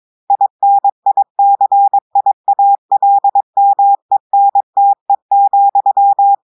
Morse code, heard as a keyed tone, about 22 wpm, 800 Hz. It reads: INICIALMENTE,
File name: audios / portugais / portugais texte en clair 2 vitesse 22 mots minutes